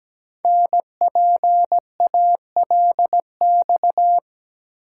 Morse code, heard as a keyed tone, about 17 wpm, 700 Hz. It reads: NPALX
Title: Code Groups - 5 Letter:Numbers - 17wpm